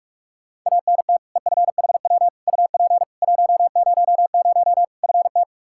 Morse code, heard as a keyed tone, about 46 wpm, 700 Hz. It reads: ANT EFHW UP 100 FT
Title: Practice QSO - Exchange 2 - Info - 45wpm